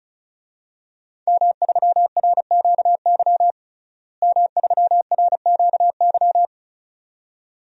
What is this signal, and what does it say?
Morse code, heard as a keyed tone, about 34 wpm, 700 Hz. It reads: M3RQY M3RQY
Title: DXCC-100 Call Signs - 5 Characters - Repeated 2x - 35wpm